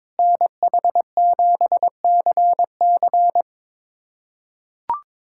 Morse code, heard as a keyed tone, about 22 wpm, 700 Hz. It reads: NH7CC
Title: USA Callsign DE Callsign - 5 Characters - 22wpm